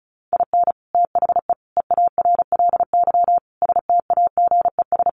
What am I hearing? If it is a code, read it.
Morse code, heard as a keyed tone, about 34 wpm, 700 Hz. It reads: IN THE EARLY STAGES
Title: Sets of 5 Words Encore 35wpm